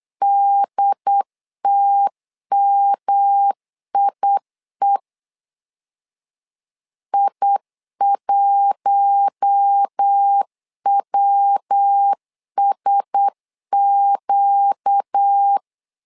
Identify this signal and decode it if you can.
Morse code, heard as a keyed tone, about 8 wpm, 800 Hz. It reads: DTMIE I1WSQ
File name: LEZIONI / 4 Q1S 8wpm 2710